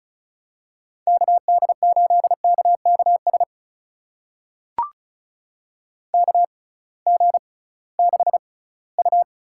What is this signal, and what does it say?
Morse code, heard as a keyed tone, about 36 wpm, spaced as for 20 wpm, 700 Hz. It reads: KD8KKS KG6U
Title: USA Call Signs - 6 Characters - 6x Character Spacing - 35wpm